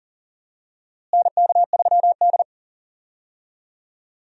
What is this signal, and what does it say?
Morse code, heard as a keyed tone, about 40 wpm, 700 Hz. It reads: NK3D
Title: USA Call Signs - 4 Characters - Repeated 2x - 40wpm